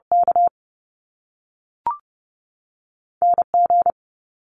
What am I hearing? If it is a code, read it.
Morse code, heard as a keyed tone, about 30 wpm, 700 Hz. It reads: K N G
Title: Top Two Letter Combinations in Words 30wpm